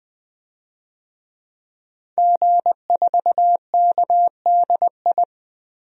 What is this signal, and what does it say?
Morse code, heard as a keyed tone, about 20 wpm, 700 Hz. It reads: G4KDI
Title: DXCC-100 Call Signs - 5 Characters - Repeated 3x - 20wpm